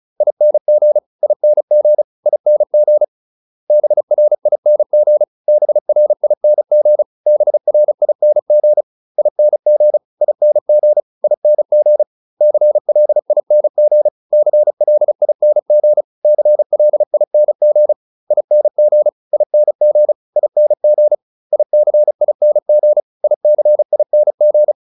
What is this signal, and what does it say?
Morse code, heard as a keyed tone, about 34 wpm, 600 Hz. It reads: ING ING ING BRING BRING BRING ING ING ING CLING CLING CLING ING ING ING ICING ICING